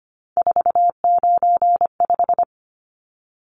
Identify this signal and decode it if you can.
Morse code, heard as a keyed tone, about 25 wpm, 700 Hz. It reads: 495
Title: Numbers - 3-Digit 25wpm